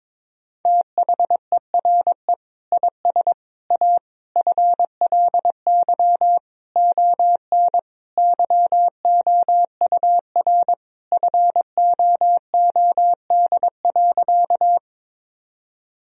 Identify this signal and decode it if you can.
Morse code, heard as a keyed tone, about 22 wpm, 700 Hz. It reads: THERE IS A FLY ON YOUR FOOD.
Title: Sentences from Top 1000 Words - Review Part 2 - 22wpm